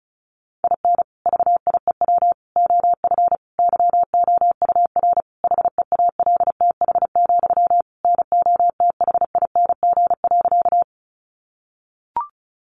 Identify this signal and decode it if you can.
Morse code, heard as a keyed tone, about 34 wpm, 700 Hz. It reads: IN VIEW OF YOUR HEALTH, NOTHING.
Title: Sentences from Top 900 Words 35wpm